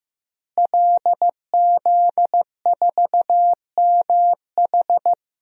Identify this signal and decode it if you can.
Morse code, heard as a keyed tone, about 15 wpm, 700 Hz. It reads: LZ4MH
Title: DXCC-100 Call Signs - 5 Characters - Repeated 3x - 15wpm